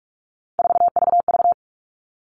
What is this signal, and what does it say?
Morse code, heard as a keyed tone, about 44 wpm, 700 Hz. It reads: 4VV
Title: All Cut Numbers - 3-Digit 45wpm